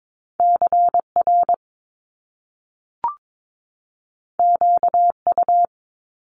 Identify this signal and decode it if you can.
Morse code, heard as a keyed tone, about 22 wpm, 700 Hz. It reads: CR QU
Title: Top Two Letter Combinations in Words 22wpm